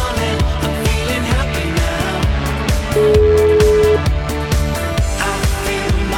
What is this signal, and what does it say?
Ringback tone (Europe, 425 Hz), with noise.